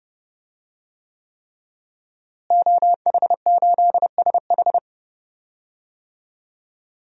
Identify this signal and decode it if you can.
Morse code, heard as a keyed tone, about 30 wpm, 700 Hz. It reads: OH8SH
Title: DXCC-100 Call Signs - 5 Characters - Repeated 3x - 30wpm